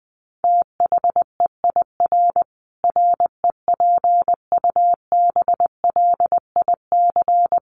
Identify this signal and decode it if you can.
Morse code, heard as a keyed tone, about 20 wpm, 700 Hz. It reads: THEIR REPUBLIC
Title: Sentences from Top 1500 Words 20wpm